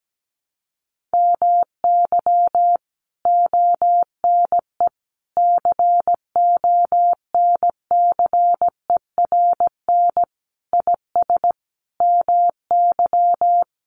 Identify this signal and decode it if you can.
Morse code, heard as a keyed tone, about 17 wpm, 700 Hz. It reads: MY ONE CONCERN IS MY
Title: Sentences from Top 1200 Words 17wpm